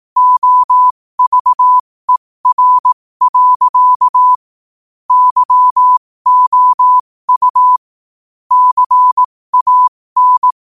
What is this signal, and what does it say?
Morse code, heard as a keyed tone, about 18 wpm, spaced as for 15 wpm, 1000 Hz. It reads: OVER. YOU CAN
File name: qotdfiles / 15wpm